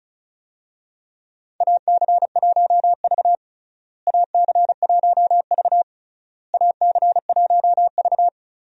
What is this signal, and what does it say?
Morse code, heard as a keyed tone, about 34 wpm, 700 Hz. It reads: AC1V AC1V AC1V
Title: USA Call Signs - 4 Characters - Repeated 3x - 35wpm